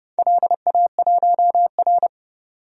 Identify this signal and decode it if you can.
Morse code, heard as a keyed tone, about 31 wpm, 700 Hz. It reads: LA1R
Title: DXCC-100 Call Signs - 4 Characters - 30wpm